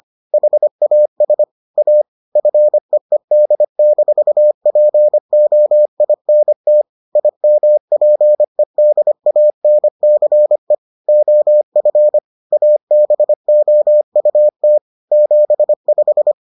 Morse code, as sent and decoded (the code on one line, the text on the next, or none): .... .- ... / .- / ..-. . -.. -....- .--. --- .. -. - / .. -- .--. -.. .- -. -.-. / --- ..-. / .- -... --- ..- - / --... .....
HAS A FEED-POINT IMPEDANCE OF ABOUT 75